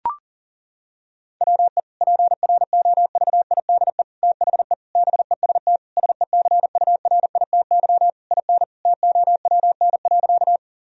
Morse code, heard as a keyed tone, about 37 wpm, 700 Hz. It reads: WE PROVIDE THE BEST SECURITY IN TOWN.
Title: Sentences from Top 1000 Words - Review Part 4 - 40wpm